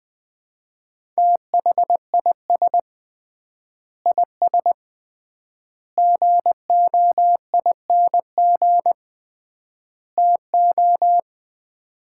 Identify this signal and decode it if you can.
Morse code, heard as a keyed tone, about 21 wpm, 700 Hz. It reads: THIS IS GOING TO
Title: 3x Word-Spacing - Sets of 4 Words - 20wpm